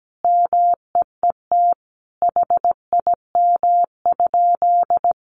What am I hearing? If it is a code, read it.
Morse code, heard as a keyed tone, about 17 wpm, 700 Hz. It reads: MEET HIM?
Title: Sentences from Top 1100 Words 17wpm